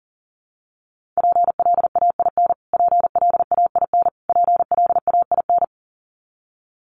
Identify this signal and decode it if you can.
Morse code, heard as a keyed tone, about 40 wpm, 700 Hz. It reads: PLAIN PLAIN PLAIN
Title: Top 500 Words Repeated 40wpm